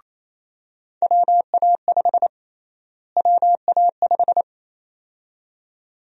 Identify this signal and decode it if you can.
Morse code, heard as a keyed tone, about 28 wpm, 700 Hz. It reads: WA5 WA5
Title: USA Call Sign Prefixes - Repeated 2x - 28wpm